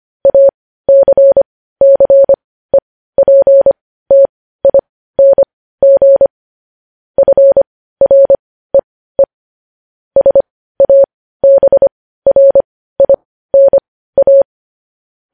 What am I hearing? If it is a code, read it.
Morse code, heard as a keyed tone, about 25 wpm, spaced as for 15 wpm, 550 Hz. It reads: ACCEPTING FREE SABRINA
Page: news-headlines-in-morse-code-at-15-wpm